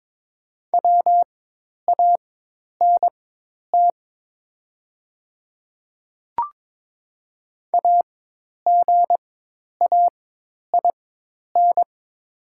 Morse code, heard as a keyed tone, about 22 wpm, spaced as for 10 wpm, 700 Hz. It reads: WANT AGAIN
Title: 4x Character Spacing - Top 300 Words - 22wpm